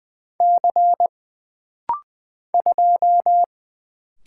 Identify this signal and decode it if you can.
Morse code, heard as a keyed tone, about 20 wpm, 700 Hz. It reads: C 2
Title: Single Letter-Number - Rapid-Fire 20wpm